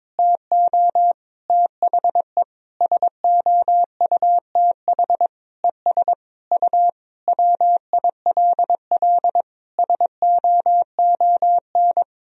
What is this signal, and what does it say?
Morse code, heard as a keyed tone, about 21 wpm, 700 Hz. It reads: TO THE SOUTH ES U WILL SOON
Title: Sentences from Top 400 Words and Abbreviations 22wpm